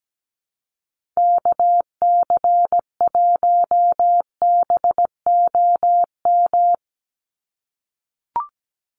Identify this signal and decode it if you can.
Morse code, heard as a keyed tone, about 17 wpm, 700 Hz. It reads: KC1BOM
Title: USA Call Signs 17wpm